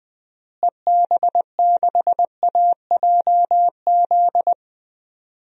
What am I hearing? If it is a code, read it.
Morse code, heard as a keyed tone, about 21 wpm, 700 Hz. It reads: EB6AJZ
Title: DXCC-100 Call Signs - 6 Characters - Repeated 3x - 20wpm